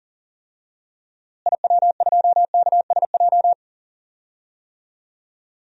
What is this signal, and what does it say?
Morse code, heard as a keyed tone, about 37 wpm, 700 Hz. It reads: IW2KSJ
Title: DXCC-100 Call Signs - 6 Characters - Repeated 2x - 40wpm